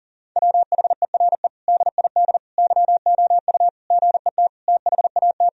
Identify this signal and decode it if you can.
Morse code, heard as a keyed tone, about 40 wpm, 700 Hz. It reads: WHERE DID YOU GET THAT
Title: Sentences from Top 1000 Words - Review Part 3 - 40wpm